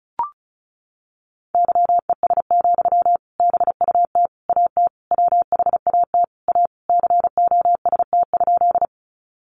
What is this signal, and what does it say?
Morse code, heard as a keyed tone, about 34 wpm, 700 Hz. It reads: YES, BUT AT WHAT A COST?
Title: Sentences from Top 800 Words 35wpm